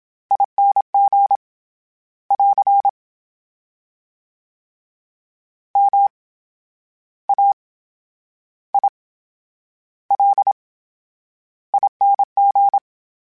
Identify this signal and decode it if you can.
Morse code, heard as a keyed tone, about 27 wpm, spaced as for 11 wpm, 800 Hz. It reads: ING <AR> MAILING
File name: Special INGs / Special INGs-mp3s / VerySlowToFaster Ing @25